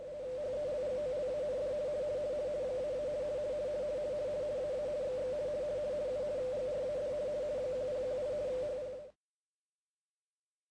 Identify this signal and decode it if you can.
RTTY, about 50 baud. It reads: HELGOLAND -- AA XX U GRAD 1018 HPA
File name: attachment / Empfaenger20 240kHz